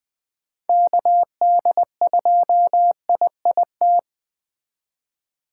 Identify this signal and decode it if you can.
Morse code, heard as a keyed tone, about 24 wpm, 700 Hz. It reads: KD2IIT
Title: Speed Racing - USA 2x3 Call Signs 20wpm